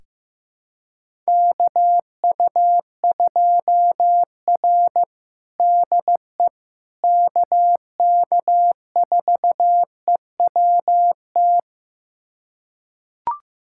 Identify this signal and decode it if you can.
Morse code, heard as a keyed tone, about 15 wpm, 700 Hz. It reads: KU2R DE KK4EWT
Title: USA Callsign DE Callsign 15wpm